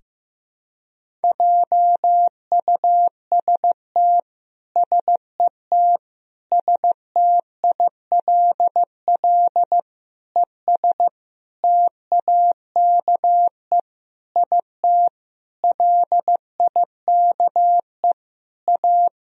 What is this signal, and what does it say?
Morse code, heard as a keyed tone, about 15 wpm, 700 Hz. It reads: JUST SET STILL ES TAKE IT LIKE A
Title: Sentences from Top 300 Words and Abbreviations 15wpm